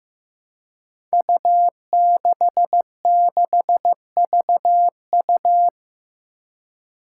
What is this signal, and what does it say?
Morse code, heard as a keyed tone, about 15 wpm, 700 Hz. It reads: U66VU
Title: Five Challenging Letter-Number Combinations 15wpm